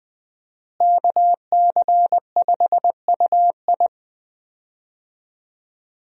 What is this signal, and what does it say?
Morse code, heard as a keyed tone, about 20 wpm, 700 Hz. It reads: KC5UI
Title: USA Call Signs - 5 Characters - Repeated 3x - 20wpm